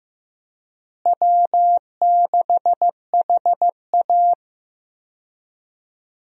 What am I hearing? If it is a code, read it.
Morse code, heard as a keyed tone, about 15 wpm, 700 Hz. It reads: W6HA
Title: USA Call Signs - 4 Characters - Repeated 3x - 15wpm